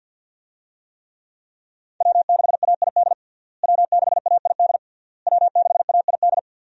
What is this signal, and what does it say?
Morse code, heard as a keyed tone, about 50 wpm, 700 Hz. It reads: W6AID W6AID W6AID
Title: USA Call Signs - 5 Characters - Repeated 3x - 50wpm